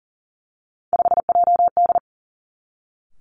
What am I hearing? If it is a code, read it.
Morse code, heard as a keyed tone, about 42 wpm, 700 Hz. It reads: 5JD
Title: Three Challenging Letter-Number Combinations 40wpm